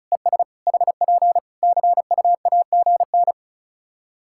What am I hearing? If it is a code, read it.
Morse code, heard as a keyed tone, about 34 wpm, 700 Hz. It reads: ES HP CUAGN
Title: Practice QSO - Exchange 4 - Ending - 35wpm